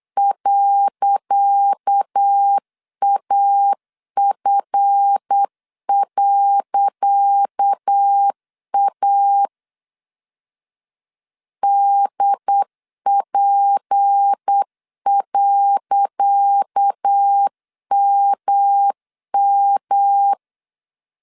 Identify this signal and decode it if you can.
Morse code, heard as a keyed tone, about 8 wpm, 800 Hz. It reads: .AF.A DP.MM
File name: LEZIONI / 10 SIGNS / 10C DOT 8wpm 3649